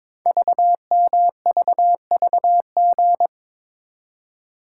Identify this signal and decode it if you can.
Morse code, heard as a keyed tone, about 22 wpm, 700 Hz. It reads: VMVVG
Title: Code Groups - 5 Letters - 22wpm